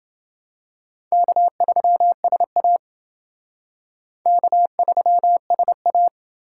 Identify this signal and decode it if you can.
Morse code, heard as a keyed tone, about 28 wpm, 700 Hz. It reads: K3SA K3SA